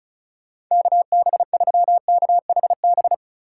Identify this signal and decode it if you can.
Morse code, heard as a keyed tone, about 42 wpm, 700 Hz. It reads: KB3KHB
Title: Speed Racing - USA Call Signs 35wpm